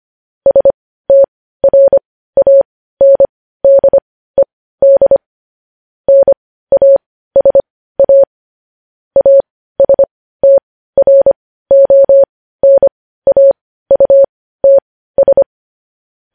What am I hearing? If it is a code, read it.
Morse code, heard as a keyed tone, about 25 wpm, spaced as for 15 wpm, 550 Hz. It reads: STRANDED NASA ASTRONAUTS
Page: news headlines in morse code at 15 wpm